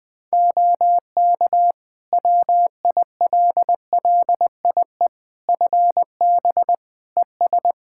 Morse code, heard as a keyed tone, about 20 wpm, 700 Hz. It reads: OK WILLIE FB ES
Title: Practice QSO - Exchange 2 - Info - 20wpm